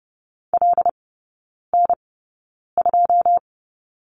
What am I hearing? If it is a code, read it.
Morse code, heard as a keyed tone, about 30 wpm, spaced as for 9 wpm, 700 Hz. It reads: LN2
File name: 081 Lesson 14e - Number 2 - Three Characters 30wpm